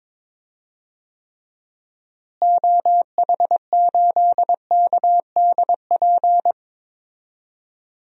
Morse code, heard as a keyed tone, about 22 wpm, 700 Hz. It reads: OH8KDP
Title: DXCC-100 Call Signs - 6 Characters - Repeated 2x - 22wpm